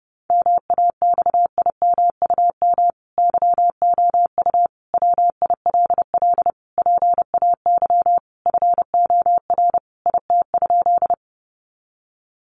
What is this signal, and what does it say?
Morse code, heard as a keyed tone, about 28 wpm, 700 Hz. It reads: MAXIMUM YOU WILL PAY FOR IT?
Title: Sentences from Top 1300 Words 30wpm